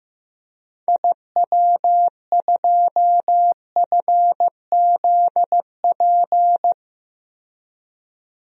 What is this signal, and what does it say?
Morse code, heard as a keyed tone, about 15 wpm, 700 Hz. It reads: IW2FZP
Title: DXCC-100 Call Signs - 6 Characters - 15wpm